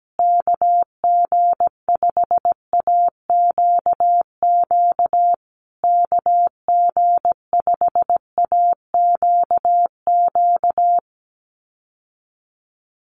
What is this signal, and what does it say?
Morse code, heard as a keyed tone, about 17 wpm, 700 Hz. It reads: KG5AQQ KG5AQQ
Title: USA Call Signs Repeated 3x 17wpm